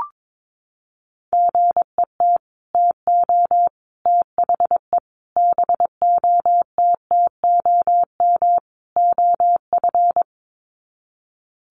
Morse code, heard as a keyed tone, about 22 wpm, 700 Hz. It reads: GET TO THE BOTTOM OF
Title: Sets of 5 Words Encore 22wpm